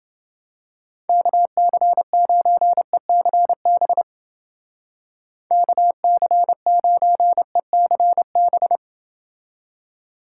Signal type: Morse code

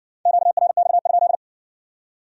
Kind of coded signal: Morse code

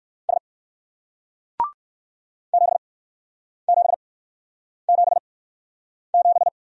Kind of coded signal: Morse code

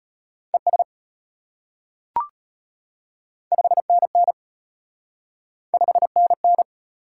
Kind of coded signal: Morse code

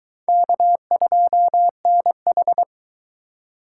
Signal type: Morse code

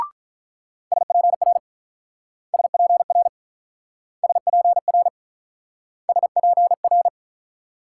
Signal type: Morse code

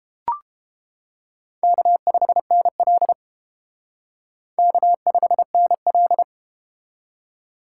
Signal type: Morse code

